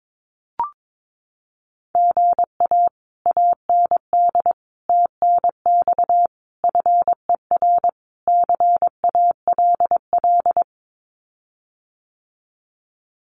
Morse code, heard as a keyed tone, about 22 wpm, 700 Hz. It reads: GA AND TNX FER CALL
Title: Common QSO Phrases Encore 22wpm